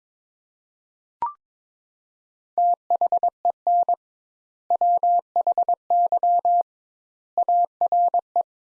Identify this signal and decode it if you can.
Morse code, heard as a keyed tone, about 22 wpm, 700 Hz. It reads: THEN WHY ARE
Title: Sentences Top 200 Words - 22wpm